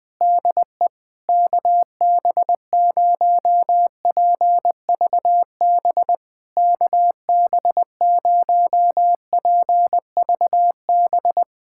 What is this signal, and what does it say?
Morse code, heard as a keyed tone, about 20 wpm, 700 Hz. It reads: DE KB0PVB KB0PVB
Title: POTA QSO Practice - 20wpm